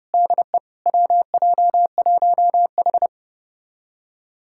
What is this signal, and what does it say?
Morse code, heard as a keyed tone, about 30 wpm, 700 Hz. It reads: DE WJ1H
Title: USA Callsign DE Callsign - 4 Characters - 30wpm